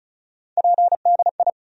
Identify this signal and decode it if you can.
Morse code, heard as a keyed tone, about 36 wpm, 700 Hz. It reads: PDI
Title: USA Call Sign Postfixes - 3 Letters 35wpm